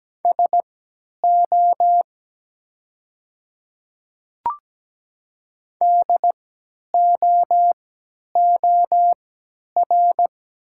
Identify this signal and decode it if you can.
Morse code, heard as a keyed tone, about 17 wpm, spaced as for 10 wpm, 700 Hz. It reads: SO DOOR
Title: 3x Character Spacing - Top 300 Words - 17wpm